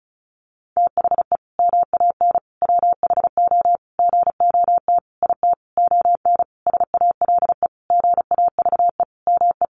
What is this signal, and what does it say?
Morse code, heard as a keyed tone, about 34 wpm, 700 Hz. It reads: THE MAN WHO GOT IT ON SALE GAVE ME
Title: Sentences from Top 1400 Words 35wpm